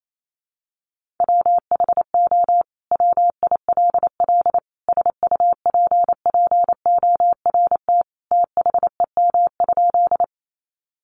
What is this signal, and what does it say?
Morse code, heard as a keyed tone, about 28 wpm, 700 Hz. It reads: WHO WILL SUPPORT THEM?
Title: Sentences from Top 800 Words 28wpm